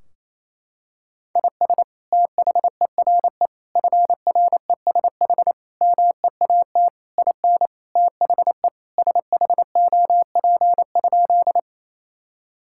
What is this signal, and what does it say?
Morse code, heard as a keyed tone, about 28 wpm, 700 Hz. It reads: IS THERE FRESH MEAT IN THE SHOP?
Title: Sentences from Top 1600 Words 28wpm